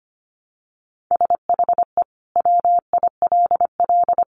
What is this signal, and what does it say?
Morse code, heard as a keyed tone, about 25 wpm, 700 Hz. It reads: SHE WILL
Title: Sentences from Top 1200 Words 25wpm